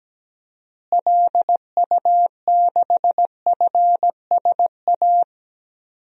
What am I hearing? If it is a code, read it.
Morse code, heard as a keyed tone, about 17 wpm, 700 Hz. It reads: LU6FSA
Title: DXCC-100 Call Signs - 6 Characters - Repeated 3x - 17wpm